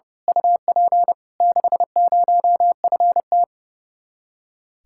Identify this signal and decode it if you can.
Morse code, heard as a keyed tone, about 30 wpm, 700 Hz. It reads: UP 60FT
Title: Speed-Racing Common QSO Phrases - 30wpm